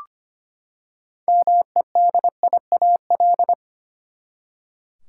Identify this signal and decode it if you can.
Morse code, heard as a keyed tone, about 25 wpm, 700 Hz. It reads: MEDIAL
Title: Practice QSO - Exchange 3 - More Info - 25wpm